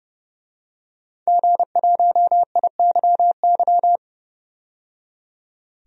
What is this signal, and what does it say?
Morse code, heard as a keyed tone, about 31 wpm, 700 Hz. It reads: G1IYY
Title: DXCC-100 Call Signs - 5 Characters - 30wpm